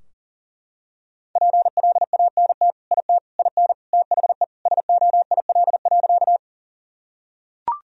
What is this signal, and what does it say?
Morse code, heard as a keyed tone, about 40 wpm, 700 Hz. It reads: PLANT IT IN THE SOIL.
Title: Sentences from Top 1000 Words - Review Part 4 - 40wpm